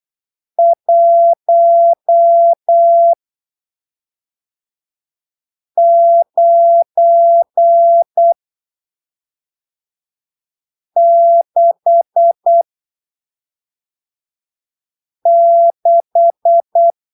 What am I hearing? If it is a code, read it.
Morse code, heard as a keyed tone, about 8 wpm, spaced as for 3 wpm, 680 Hz. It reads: 1966
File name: arquivos / texto / texto 07 08ppm